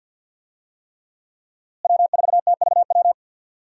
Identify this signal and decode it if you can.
Morse code, heard as a keyed tone, about 51 wpm, 700 Hz. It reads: W4TVW
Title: USA Call Signs - 5 Characters - Repeated 3x - 50wpm